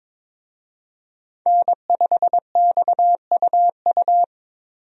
Morse code, heard as a keyed tone, about 22 wpm, 700 Hz. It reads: N5XUU
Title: USA Call Signs - 5 Characters - Repeated 2x - 22wpm